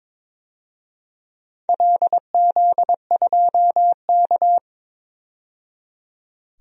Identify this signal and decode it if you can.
Morse code, heard as a keyed tone, about 22 wpm, 700 Hz. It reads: LZ2K
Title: DXCC-100 Call Signs - 4 Characters - 22wpm